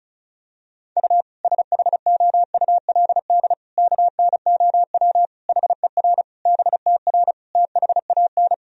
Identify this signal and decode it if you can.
Morse code, heard as a keyed tone, about 34 wpm, 700 Hz. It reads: U SHOULD KNOW HER BTR THAN
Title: Sentences from Top 400 Words and Abbreviations 35wpm